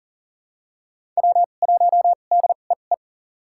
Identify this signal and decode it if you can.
Morse code, heard as a keyed tone, about 40 wpm, spaced as for 29 wpm, 700 Hz. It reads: W1DEE
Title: USA Call Signs - 2x Character Spacing - 40wpm